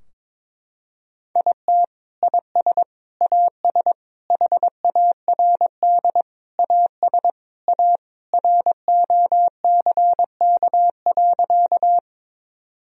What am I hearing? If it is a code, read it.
Morse code, heard as a keyed tone, about 22 wpm, 700 Hz. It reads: IT IS AS HARD AS A ROCK.
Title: Sentences from Top 1000 Words - Review Part 2 - 22wpm